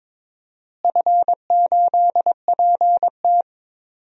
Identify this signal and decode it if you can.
Morse code, heard as a keyed tone, about 22 wpm, 700 Hz. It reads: F8PT
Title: DXCC-100 Call Signs - 4 Characters - Repeated 3x - 22wpm